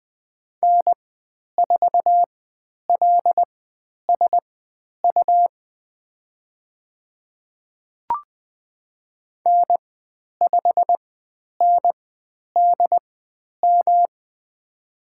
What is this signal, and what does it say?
Morse code, heard as a keyed tone, about 20 wpm, spaced as for 10 wpm, 700 Hz. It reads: N4LSU N5NDM
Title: 109 Lesson 18g - Letter m - Call Signs 20wpm